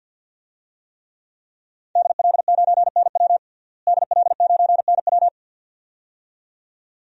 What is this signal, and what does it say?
Morse code, heard as a keyed tone, about 52 wpm, 700 Hz. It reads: DL9NW DL9NW
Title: DXCC-100 Call Signs - 5 Characters - Repeated 2x - 50wpm